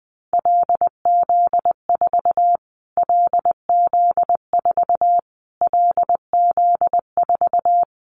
Morse code, heard as a keyed tone, about 20 wpm, 700 Hz. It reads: LZ4 LZ4 LZ4
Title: Callsign Prefixes for Top 25 DXCC - Repeated 3x - 20wpm